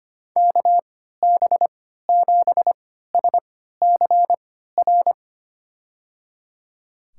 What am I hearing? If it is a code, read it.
Morse code, heard as a keyed tone, about 25 wpm, spaced as for 14 wpm, 700 Hz. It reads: KB7SCR
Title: USA Call Signs - 3x Character Spacing - 25wpm